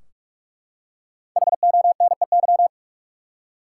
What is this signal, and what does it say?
Morse code, heard as a keyed tone, about 42 wpm, 700 Hz. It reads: HONEY
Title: Five Letter Words - 45wpm